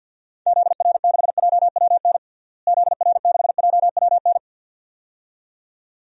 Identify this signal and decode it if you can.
Morse code, heard as a keyed tone, about 46 wpm, 700 Hz. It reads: ZR6JWN ZR6JWN
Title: DXCC-100 Call Signs - 6 Characters - Repeated 2x - 50wpm